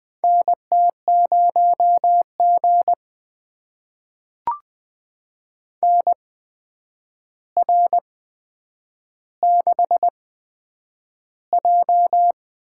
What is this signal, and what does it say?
Morse code, heard as a keyed tone, about 20 wpm, spaced as for 9 wpm, 700 Hz. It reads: NT0G NR6J
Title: USA Call Signs - 4 Characters - 8x Character Spacing - 20wpm